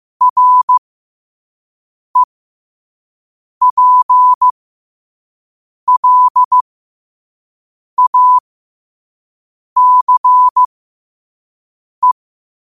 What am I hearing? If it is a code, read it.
Morse code, heard as a keyed tone, about 15 wpm, spaced as for 5 wpm, 1000 Hz. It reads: REPLACE
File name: qotdfiles / 5wpm